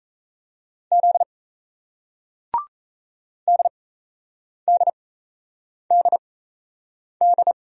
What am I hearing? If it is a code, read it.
Morse code, heard as a keyed tone, about 38 wpm, spaced as for 8 wpm, 700 Hz. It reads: Z DDDD